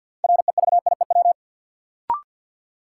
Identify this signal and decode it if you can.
Morse code, heard as a keyed tone, about 53 wpm, 700 Hz. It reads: REVIEW
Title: Sentences from Top 900 Words - New Word Drill - 50wpm